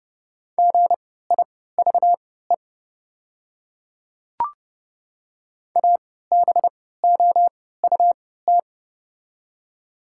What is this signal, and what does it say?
Morse code, heard as a keyed tone, about 30 wpm, spaced as for 17 wpm, 700 Hz. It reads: GIVE ABOUT
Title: 3x Character Spacing - Top 300 Words - 30wpm